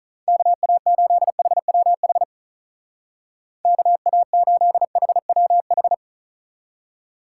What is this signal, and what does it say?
Morse code, heard as a keyed tone, about 45 wpm, 700 Hz. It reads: KA8HWH KA8HWH